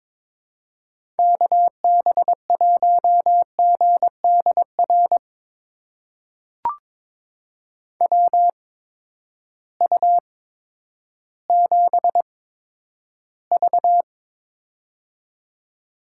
Morse code, heard as a keyed tone, about 22 wpm, spaced as for 11 wpm, 700 Hz. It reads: KB1GDR WU7V
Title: USA Call Signs - 8x Character Spacing - 22wpm